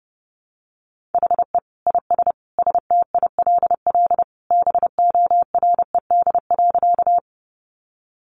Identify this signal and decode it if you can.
Morse code, heard as a keyed tone, about 30 wpm, 700 Hz. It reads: HE IS STILL BORED.
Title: Easy Phrases 30wpm